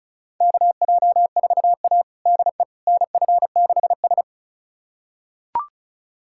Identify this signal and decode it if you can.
Morse code, heard as a keyed tone, about 36 wpm, 700 Hz. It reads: KJ4A DE NF6S
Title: USA Callsign DE Callsign - 4 Characters - 35wpm